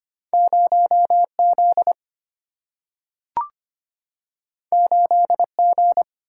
Morse code, heard as a keyed tone, about 25 wpm, 700 Hz. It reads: 0Z 8G
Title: Single-Letter Single-Number 25wpm